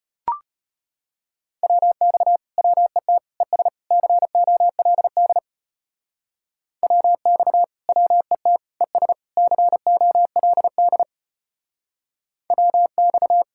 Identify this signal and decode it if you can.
Morse code, heard as a keyed tone, about 33 wpm, 700 Hz. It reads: WX WET ES COLD WX WET ES COLD WX